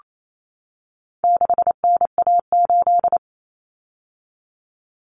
Morse code, heard as a keyed tone, about 28 wpm, 700 Hz. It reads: 6NA8
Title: Common Cut Numbers - 4-Digit 28wpm